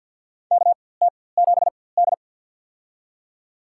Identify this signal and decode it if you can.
Morse code, heard as a keyed tone, about 50 wpm, spaced as for 23 wpm, 700 Hz. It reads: KT7D
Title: USA Call Signs - 4x Character Spacing - 50wpm